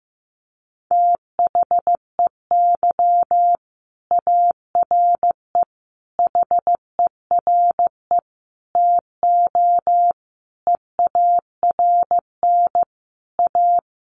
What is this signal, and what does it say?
Morse code, heard as a keyed tone, about 15 wpm, 700 Hz. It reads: THEY ARE HERE TO EARN A
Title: Sentences from Top 1500 Words 15wpm